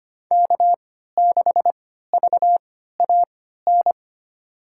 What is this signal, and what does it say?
Morse code, heard as a keyed tone, about 25 wpm, spaced as for 14 wpm, 700 Hz. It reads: K6VAN
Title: USA Call Signs - 3x Character Spacing - 25wpm